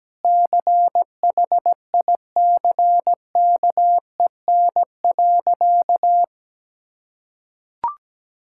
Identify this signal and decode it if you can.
Morse code, heard as a keyed tone, about 17 wpm, 700 Hz. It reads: CHICKEN.
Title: Sentences from Top 1000 Words - Review Part 5 - 17wpm